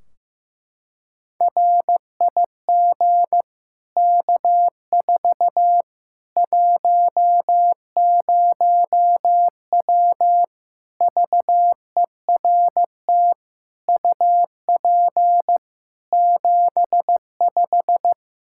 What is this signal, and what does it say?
Morse code, heard as a keyed tone, about 15 wpm, 700 Hz. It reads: RIG K4 10W VERT UP 75
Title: Practice QSO - Exchange 2 - Info - 15wpm